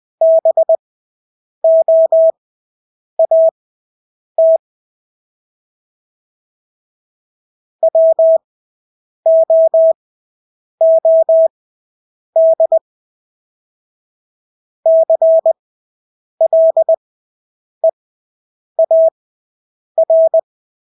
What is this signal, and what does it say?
Morse code, heard as a keyed tone, about 20 wpm, spaced as for 8 wpm, 650 Hz. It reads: BOAT WOOD CLEAR
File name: Engelske ord 20/8 wpm